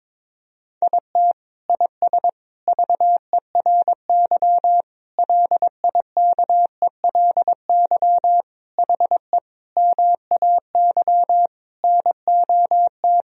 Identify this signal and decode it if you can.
Morse code, heard as a keyed tone, about 22 wpm, 700 Hz. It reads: IT IS VERY LIKELY HE MAY NOT
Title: Sentences from Top 1000 Words - Review Part 4 - 22wpm